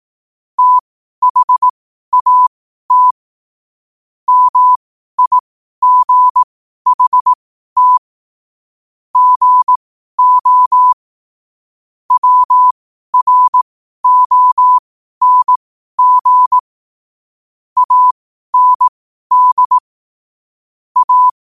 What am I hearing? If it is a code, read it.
Morse code, heard as a keyed tone, about 18 wpm, spaced as for 12 wpm, 1000 Hz. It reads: THAT MIGHT GO WRONG AND A
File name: qotdfiles / 12wpm